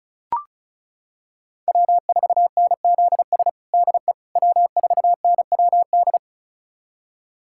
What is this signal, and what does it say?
Morse code, heard as a keyed tone, about 36 wpm, 700 Hz. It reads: W4NZS DE W4NWD
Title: USA Callsign DE Callsign - 5 Characters - 35wpm